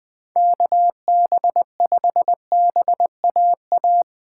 Morse code, heard as a keyed tone, about 20 wpm, 700 Hz. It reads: KB5BAA